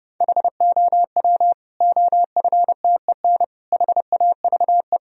Morse code, heard as a keyed tone, about 32 wpm, 700 Hz. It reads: HOW OFTEN HAVE
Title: Sentences from Top 1000 Words - Review Part 1 - 30wpm